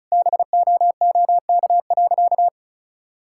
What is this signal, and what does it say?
Morse code, heard as a keyed tone, about 36 wpm, 700 Hz. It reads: BOOK.
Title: Sentences from Top 1000 Words - Review Part 5 - 35wpm